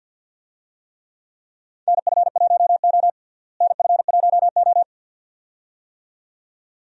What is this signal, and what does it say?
Morse code, heard as a keyed tone, about 52 wpm, 700 Hz. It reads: NF1O NF1O
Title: USA Call Signs - 4 Characters - Repeated 2x - 50wpm